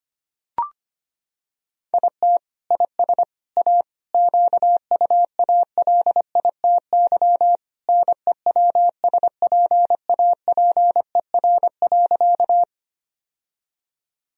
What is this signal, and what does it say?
Morse code, heard as a keyed tone, about 25 wpm, 700 Hz. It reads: IT IS A QUALITY NEWSPAPER.
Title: Sentences from Top 1000 Words - Review Part 4 - 25wpm